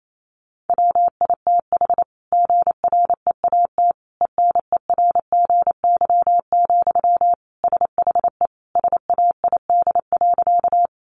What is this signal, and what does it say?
Morse code, heard as a keyed tone, about 28 wpm, 700 Hz. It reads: WITH GREAT ENERGY, SHE SAID.
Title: Sentences from Top 900 Words 28wpm